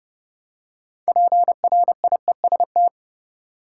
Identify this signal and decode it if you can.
Morse code, heard as a keyed tone, about 32 wpm, 700 Hz. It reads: PRIEST
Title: Six Letter Words - 30wpm